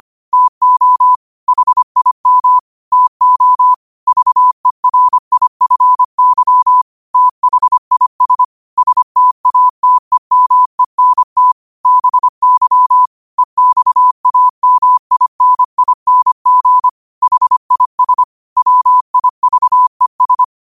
Morse code, heard as a keyed tone, about 25 wpm, 1000 Hz. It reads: TO HIM TO VERIFY THIS STATEMENT BY EXAMINING HIS WIVES
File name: qotdfiles / 25wpm